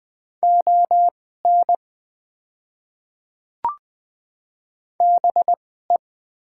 Morse code, heard as a keyed tone, about 20 wpm, spaced as for 15 wpm, 690 Hz. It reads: ON BE
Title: 2x Character Spacing - Top 100 Words - 20wpm